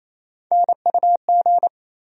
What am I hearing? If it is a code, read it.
Morse code, heard as a keyed tone, about 27 wpm, 700 Hz. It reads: NUG
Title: USA Call Sign Postfixes - 3 Letters 28wpm